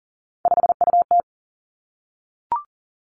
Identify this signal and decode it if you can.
Morse code, heard as a keyed tone, about 41 wpm, 700 Hz. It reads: 5UT
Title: Three Challenging Letter-Number Combinations 40wpm